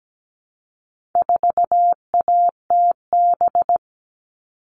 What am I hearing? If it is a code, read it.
Morse code, heard as a keyed tone, about 17 wpm, 700 Hz. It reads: 4ATB
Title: All Cut Numbers - 4-Digit 17wpm